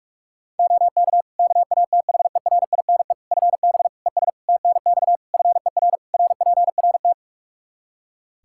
Morse code, heard as a keyed tone, about 45 wpm, 700 Hz. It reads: OK KATHERINE FB ES TNX FER RPRT
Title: K1USN SST QSO Practice 45wpm